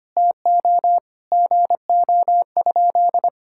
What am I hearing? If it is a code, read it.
Morse code, heard as a keyed tone, about 25 wpm, 700 Hz. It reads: TO GO?
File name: practice / 25wpm / Sentences with 2-Word Phrases - 25wpm